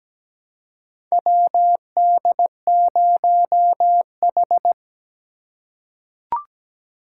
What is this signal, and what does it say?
Morse code, heard as a keyed tone, about 17 wpm, 700 Hz. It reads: WD0H
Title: USA Call Signs - 4 Characters - 2x Character Spacing - 17wpm